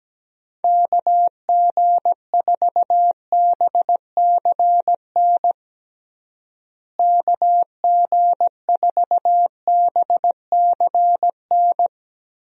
Morse code, heard as a keyed tone, about 17 wpm, 700 Hz. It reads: KG4BCN KG4BCN
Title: USA Call Signs - 6 Characters - Repeated 3x - 17wpm